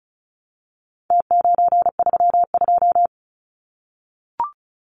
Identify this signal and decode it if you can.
Morse code, heard as a keyed tone, about 34 wpm, 700 Hz. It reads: T932
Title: Numbers - Contest Style - 4-Digit 35wpm